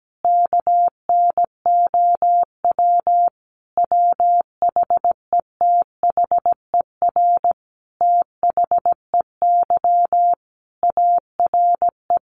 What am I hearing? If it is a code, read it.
Morse code, heard as a keyed tone, about 17 wpm, 700 Hz. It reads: KNOW WHETHER THEY ARE
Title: Sentences from Top 1500 Words 17wpm